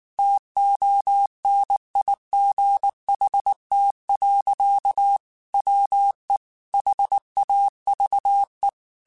Morse code, heard as a keyed tone, about 19 wpm, 800 Hz. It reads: TONIGHT. WE HAVE
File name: QSOs / QSOs-mp3s / WD0BGP 1@18